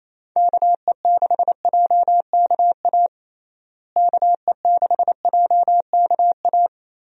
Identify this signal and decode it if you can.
Morse code, heard as a keyed tone, about 27 wpm, 700 Hz. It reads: KE6JKA KE6JKA
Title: USA Call Signs - 6 Characters - Repeated 2x - 28wpm